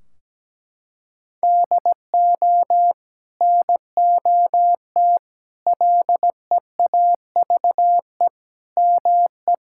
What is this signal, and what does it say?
Morse code, heard as a keyed tone, about 17 wpm, 700 Hz. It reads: DO NOT LEAVE ME
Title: Sentences from Top 1600 Words 17wpm